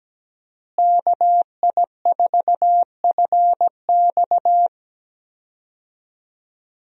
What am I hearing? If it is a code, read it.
Morse code, heard as a keyed tone, about 17 wpm, 700 Hz. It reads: KI4FX
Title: USA Call Signs - 5 Characters - Repeated 3x - 17wpm